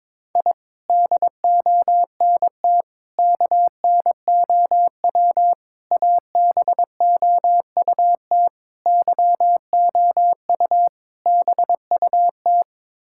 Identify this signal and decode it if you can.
Morse code, heard as a keyed tone, about 21 wpm, 700 Hz. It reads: I DONT KNOW ABOUT YOU BUT
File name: practice / 22wpm / Sets of 6 Words - 22wpm